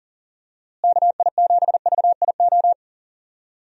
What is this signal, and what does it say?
Morse code, heard as a keyed tone, about 40 wpm, 700 Hz. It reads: KI7VIO
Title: USA Call Signs - 6 Characters - Repeated 3x - 40wpm